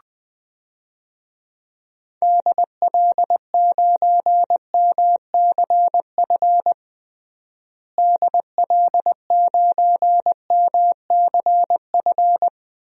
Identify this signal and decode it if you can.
Morse code, heard as a keyed tone, about 21 wpm, 700 Hz. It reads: DL9MCF DL9MCF
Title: DXCC-100 Call Signs - 6 Characters - Repeated 3x - 20wpm